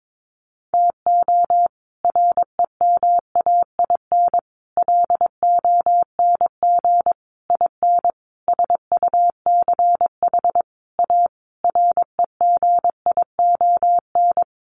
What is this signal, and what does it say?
Morse code, heard as a keyed tone, about 22 wpm, 700 Hz. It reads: TO REMAIN LONG IN SUCH A REGION
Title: Sentences from Top 1500 Words 22wpm